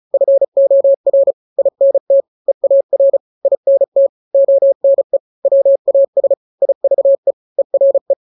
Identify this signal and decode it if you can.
Morse code, heard as a keyed tone, about 34 wpm, 550 Hz. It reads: FOR INT EAR INT ONE WAS IVE ERE
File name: 3-grams 35 30 WPM